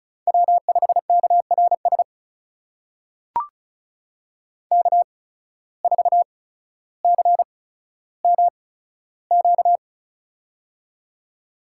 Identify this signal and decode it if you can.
Morse code, heard as a keyed tone, about 36 wpm, spaced as for 15 wpm, 700 Hz. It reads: W5KRS K4CMQ
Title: USA Call Signs - 5 Characters - 8x Character Spacing - 35wpm